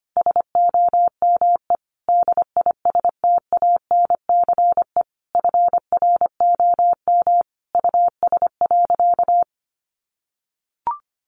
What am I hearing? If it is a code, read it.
Morse code, heard as a keyed tone, about 25 wpm, 700 Hz. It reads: SOME DISTANCE FROM US.
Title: Sentences from Top 1200 Words 25wpm